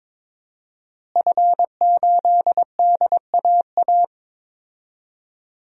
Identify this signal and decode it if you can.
Morse code, heard as a keyed tone, about 22 wpm, 700 Hz. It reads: F8DAA
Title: DXCC-100 Call Signs - 5 Characters - Repeated 3x - 22wpm